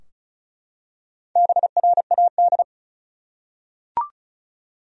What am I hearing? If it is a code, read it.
Morse code, heard as a keyed tone, about 36 wpm, 700 Hz. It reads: BRAD
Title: USA Common Names - Max 4 Letters - 35wpm